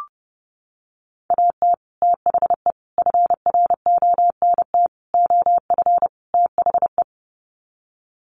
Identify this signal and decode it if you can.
Morse code, heard as a keyed tone, about 28 wpm, 700 Hz. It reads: AT THE FRONT OF THE
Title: Sets of 5 Words 30wpm